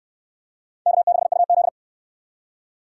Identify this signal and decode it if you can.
Morse code, heard as a keyed tone, about 64 wpm, 700 Hz. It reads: K6UZ